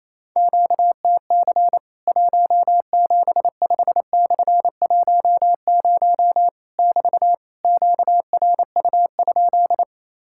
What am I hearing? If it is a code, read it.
Morse code, heard as a keyed tone, about 28 wpm, 700 Hz. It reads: QTC 175/10 <BT> QRU?
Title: Worked All Europe DX Contest - QTC Exchange - 28wpm